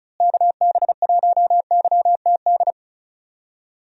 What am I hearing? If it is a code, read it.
Morse code, heard as a keyed tone, about 36 wpm, 700 Hz. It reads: KB1YTD
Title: USA Call Signs - 6 Characters - Repeated 2x - 35wpm